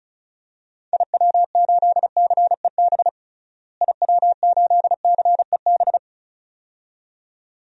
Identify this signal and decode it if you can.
Morse code, heard as a keyed tone, about 36 wpm, 700 Hz. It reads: IW8CEB IW8CEB
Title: DXCC-100 Call Signs - 6 Characters - Repeated 2x - 35wpm